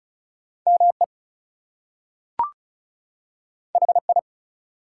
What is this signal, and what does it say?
Morse code, heard as a keyed tone, about 36 wpm, 700 Hz. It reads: ME HI
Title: Two Letter Words - 35wpm